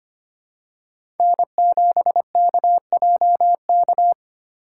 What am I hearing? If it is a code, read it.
Morse code, heard as a keyed tone, about 25 wpm, 700 Hz. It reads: N7KJK
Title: USA Call Signs - 6x Character Spacing - 25wpm